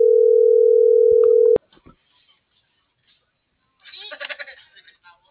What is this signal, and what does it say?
Ringback tone (North America), 440 + 480 Hz.